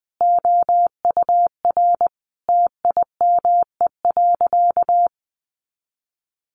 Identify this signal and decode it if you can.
Morse code, heard as a keyed tone, about 20 wpm, 700 Hz. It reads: OUR TIME.
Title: Sentences from Top 200 Words 20wpm